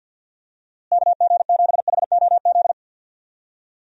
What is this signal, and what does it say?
Morse code, heard as a keyed tone, about 52 wpm, 700 Hz. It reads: KG7HOZ
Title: USA Call Signs - 6 Characters - Repeated 3x - 50wpm